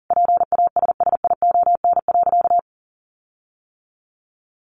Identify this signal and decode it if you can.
Morse code, heard as a keyed tone, about 40 wpm, 700 Hz. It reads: PASSION.
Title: Sentences from Top 1300 Words 40wpm